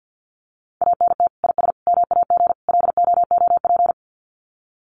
Morse code, heard as a keyed tone, about 50 wpm, 700 Hz. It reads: ANT IS MAG LOOP
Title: Common QSO Phrases 50wpm